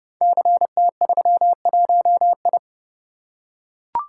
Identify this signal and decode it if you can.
Morse code, heard as a keyed tone, about 30 wpm, 700 Hz. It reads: CT31I
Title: Code Groups - 5 Letter:Numbers - 30wpm